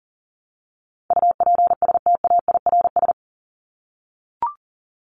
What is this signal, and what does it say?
Morse code, heard as a keyed tone, about 40 wpm, 700 Hz. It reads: UPSTAIRS
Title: Top 1751-2000 Words 40wpm